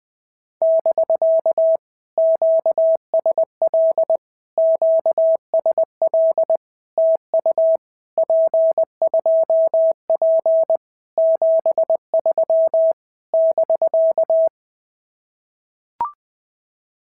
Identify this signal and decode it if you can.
Morse code, heard as a keyed tone, about 20 wpm, 650 Hz. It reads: <BK> QSL QSL TU P2P 73 <BK>
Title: POTA QSO Practice - 20wpm